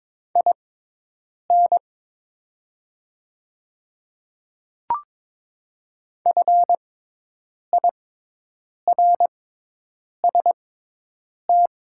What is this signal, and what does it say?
Morse code, heard as a keyed tone, about 22 wpm, spaced as for 8 wpm, 700 Hz. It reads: IN FIRST
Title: 6x Character Spacing - Top 500 Words - 22wpm